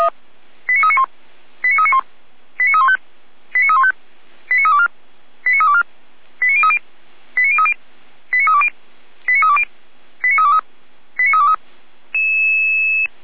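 Five-tone selective call (ZVEI), heard as quick runs of five tones, tones of about 70 ms. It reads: 89391 89391 89315 89315 89324 89324 89030 89030 89320 89320 89323 89323
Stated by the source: Five-tone ZVEI selective call received on an SDR.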